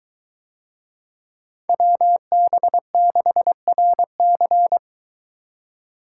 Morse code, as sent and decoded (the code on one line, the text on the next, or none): .-- -... -.... .-. -.-.
WB6RC